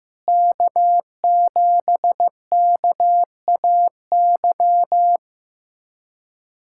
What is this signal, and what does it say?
Morse code, heard as a keyed tone, about 15 wpm, 700 Hz. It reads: K7KAY
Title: USA Callsign DE Callsign - 5 Characters - 15wpm